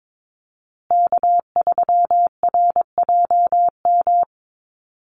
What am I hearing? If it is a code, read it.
Morse code, heard as a keyed tone, about 22 wpm, 700 Hz. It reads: K3RJM
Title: USA Call Signs Repeated 2x 22wpm